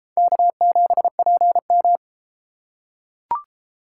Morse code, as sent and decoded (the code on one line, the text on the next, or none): -.- --... .--. --
K7PM